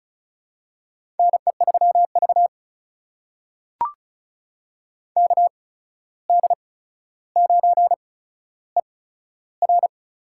Morse code, heard as a keyed tone, about 36 wpm, spaced as for 14 wpm, 700 Hz. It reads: NE3V KD9ER
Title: USA Call Signs - 8x Character Spacing - 35wpm